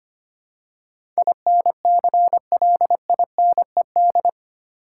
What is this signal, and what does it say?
Morse code, heard as a keyed tone, about 25 wpm, 700 Hz. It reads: INCLINED
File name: 8 Letter Words - First 3 Letters Given - 25wpm